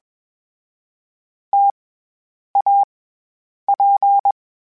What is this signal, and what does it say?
Morse code, heard as a keyed tone, about 21 wpm, 800 Hz. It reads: T A P
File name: Variable Speed / Variable Speed-mp3s / T AOEI's sp @20